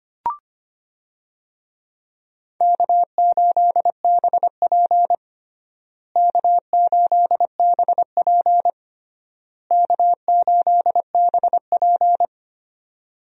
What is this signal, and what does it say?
Morse code, heard as a keyed tone, about 25 wpm, 700 Hz. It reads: K8BP K8BP K8BP
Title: USA Call Signs - 4 Characters - Repeated 3x - 25wpm